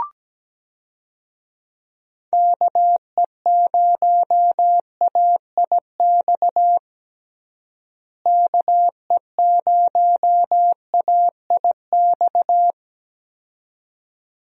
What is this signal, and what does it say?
Morse code, heard as a keyed tone, about 17 wpm, 700 Hz. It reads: KE0AIX KE0AIX
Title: USA Call Signs - 6 Characters - Repeated 2x - 17wpm